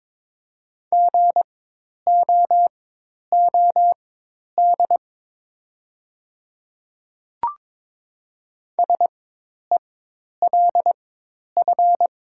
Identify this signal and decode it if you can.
Morse code, heard as a keyed tone, about 22 wpm, spaced as for 10 wpm, 700 Hz. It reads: GOOD SELF
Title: 4x Character Spacing - Top 300 Words - 22wpm